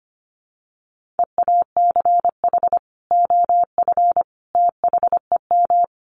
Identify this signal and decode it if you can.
Morse code, heard as a keyed tone, about 25 wpm, 700 Hz. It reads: EACH OF THEM
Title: Sentences from Top 1300 Words 25wpm